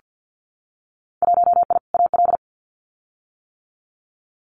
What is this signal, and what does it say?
Morse code, heard as a keyed tone, about 48 wpm, 700 Hz. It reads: 1I AL
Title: Winter Field Day Exchanges 50wpm